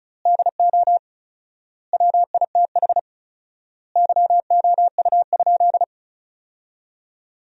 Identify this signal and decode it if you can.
Morse code, heard as a keyed tone, about 34 wpm, 700 Hz. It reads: DO WITH YOU?
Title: 4x Word-Spacing - Sentences from Top 100 Words - 35wpm